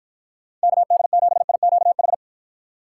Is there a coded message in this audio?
Yes: Morse code from 0.6 to 2.1 s, heard as a keyed tone, about 52 wpm, 700 Hz. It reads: KD7IQH